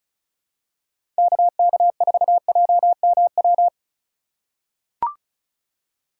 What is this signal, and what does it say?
Morse code, heard as a keyed tone, about 36 wpm, 700 Hz. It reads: KK4JMW
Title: USA Call Signs - 6 Characters - 3x Character Spacing - 35wpm